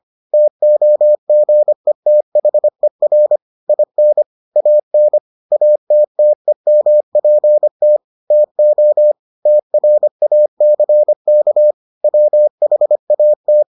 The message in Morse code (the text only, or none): TOGETHER IN AN ATTEMPT TO TRACK WHAT